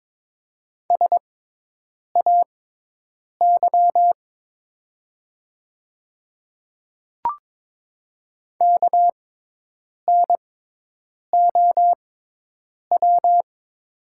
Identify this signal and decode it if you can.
Morse code, heard as a keyed tone, about 22 wpm, spaced as for 8 wpm, 700 Hz. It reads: SAY KNOW
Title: 6x Character Spacing - Top 200 Words - 22wpm